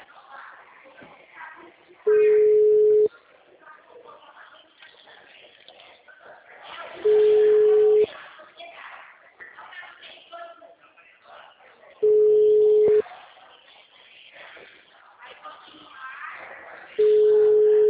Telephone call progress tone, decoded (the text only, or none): ringback tone